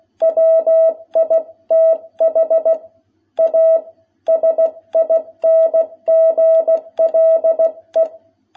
Morse code, heard as a keyed tone, about 16 wpm, 660 Hz. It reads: WITH A SINGLE
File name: jh8cbh / 2021cw